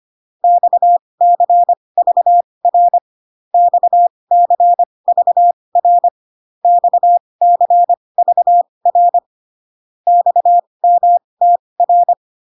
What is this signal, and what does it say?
Morse code, heard as a keyed tone, about 25 wpm, spaced as for 20 wpm, 700 Hz. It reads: XCVR XCVR XCVR XMTR